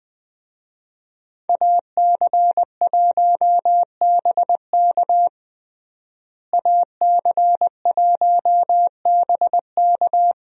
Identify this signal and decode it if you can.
Morse code, heard as a keyed tone, about 20 wpm, 700 Hz. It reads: AC1BK AC1BK
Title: USA Call Signs - 5 Characters - Repeated 3x - 20wpm